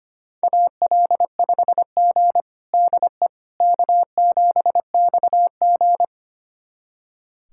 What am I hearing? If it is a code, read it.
Morse code, heard as a keyed tone, about 25 wpm, 700 Hz. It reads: AL5G DE K7XG
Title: USA Callsign DE Callsign - 4 Characters - 25wpm